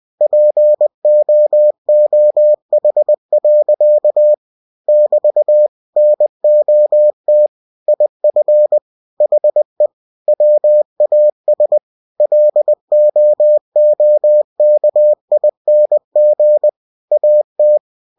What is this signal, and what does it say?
Morse code, heard as a keyed tone, about 20 wpm, 600 Hz. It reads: POOH. <BT> NOT IF HE WAS LOOKING AT
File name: cw-books / winnie-the-pooh / winnie-20wpm-page-0036